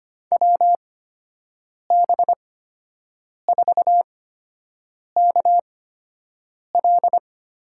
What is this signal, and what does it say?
Morse code, heard as a keyed tone, about 25 wpm, spaced as for 7 wpm, 700 Hz. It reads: WB4KL
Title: USA Call Signs - 6 Characters - 8x Character Spacing - 25wpm